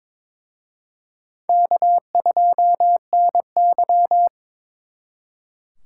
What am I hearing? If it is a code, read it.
Morse code, heard as a keyed tone, about 26 wpm, 700 Hz. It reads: K2NY
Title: Speed Racing - USA 1x2 Call Signs 22wpm